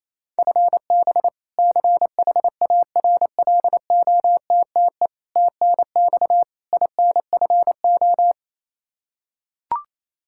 Morse code, heard as a keyed tone, about 28 wpm, 700 Hz. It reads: FB CHARLOTTE TNX INFO
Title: Practice QSO - Exchange 3 - More Info - 28wpm